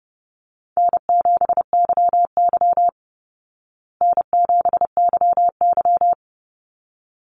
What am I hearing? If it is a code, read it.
Morse code, heard as a keyed tone, about 30 wpm, 700 Hz. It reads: N7YY N7YY
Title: Easy Call Signs Repeated 2x - 30wpm